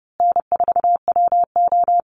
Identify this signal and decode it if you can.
Morse code, heard as a keyed tone, about 30 wpm, 700 Hz. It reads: N4WO
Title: Call Signs Repeated 2x - 30wpm